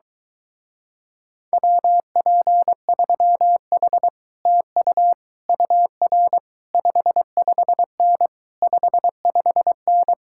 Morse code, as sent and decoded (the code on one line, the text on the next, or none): .-- .--. ...-- .... / - ..- / ..- .-. / ..... ..... -. / ..... ..... -.
WP3H TU UR 55N 55N